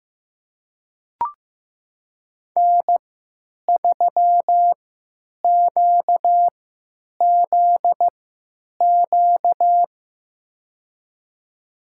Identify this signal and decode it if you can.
Morse code, heard as a keyed tone, about 15 wpm, spaced as for 9 wpm, 700 Hz. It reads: N3QZQ
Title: USA Call Signs - 5 Characters - 3x Character Spacing - 15wpm